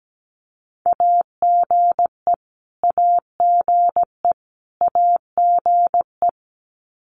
Morse code, heard as a keyed tone, about 17 wpm, 700 Hz. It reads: AGE AGE AGE
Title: Top 500 Words Repeated 17wpm